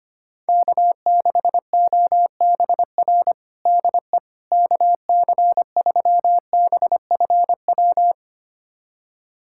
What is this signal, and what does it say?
Morse code, heard as a keyed tone, about 25 wpm, 700 Hz. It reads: K6OBR DE KC3BFW
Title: USA Callsign DE Callsign 25wpm